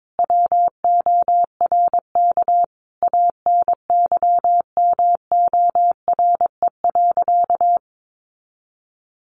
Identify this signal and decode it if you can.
Morse code, heard as a keyed tone, about 22 wpm, 700 Hz. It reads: WORK ANYMORE.
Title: Additional Easy Phrases 22wpm